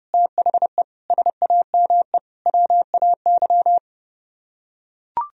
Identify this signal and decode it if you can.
Morse code, heard as a keyed tone, about 30 wpm, 700 Hz. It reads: THE SAME WAY